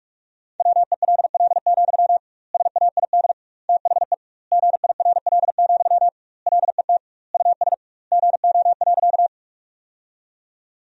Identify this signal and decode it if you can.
Morse code, heard as a keyed tone, about 42 wpm, 700 Hz. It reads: WELL, SAID THE GIRL, LET US GO.
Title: 2x Word-Spacing - Sentences from Top 400 Words - 45wpm